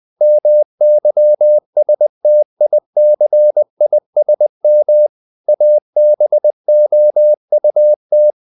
Morse code, heard as a keyed tone, about 20 wpm, 600 Hz. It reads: MYSTICISM ABOUT
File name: the old man and the sea / the old man and the sea part 1 20wpm